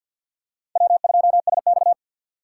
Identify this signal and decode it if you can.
Morse code, heard as a keyed tone, about 51 wpm, 700 Hz. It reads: W2SX
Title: USA Call Signs - 4 Characters - 2x Character Spacing - 50wpm